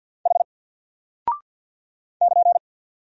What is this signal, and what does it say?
Morse code, heard as a keyed tone, about 52 wpm, 700 Hz. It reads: H <KN>
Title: Single Character (Letter:Number:Punctuation:Prosign) Alt 50wpm